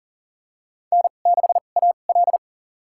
Morse code, heard as a keyed tone, about 40 wpm, spaced as for 29 wpm, 700 Hz. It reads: N6AL
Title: USA Call Signs - 2x Character Spacing - 40wpm